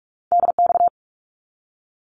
Morse code, heard as a keyed tone, about 44 wpm, 700 Hz. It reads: DX
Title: ARRL Sections 45wpm